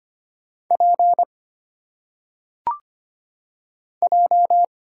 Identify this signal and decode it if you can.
Morse code, heard as a keyed tone, about 25 wpm, 700 Hz. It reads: P J
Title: Single Letters - Phonetic Pronunciation 25wpm